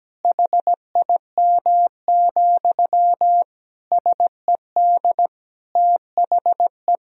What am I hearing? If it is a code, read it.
Morse code, heard as a keyed tone, about 17 wpm, 700 Hz. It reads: HIM, SED THE
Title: Sentences from Top 400 Words and Abbreviations 17wpm